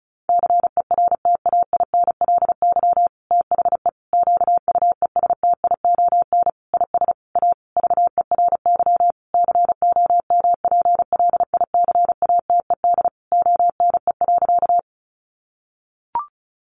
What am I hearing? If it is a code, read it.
Morse code, heard as a keyed tone, about 34 wpm, 700 Hz. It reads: CERTAINLY THE QUESTION IS A VERY COMPLICATED ONE.
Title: Sentences from Top 1200 Words 35wpm